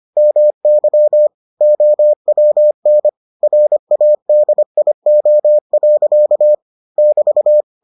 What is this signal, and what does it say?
Morse code, heard as a keyed tone, about 25 wpm, 600 Hz. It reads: MY OWN RADIO. <BT>